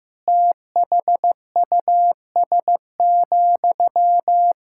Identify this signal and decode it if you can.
Morse code, heard as a keyed tone, about 15 wpm, 700 Hz. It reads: THUS,
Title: Sentences from Top 1000 Words - Review Part 5 - 15wpm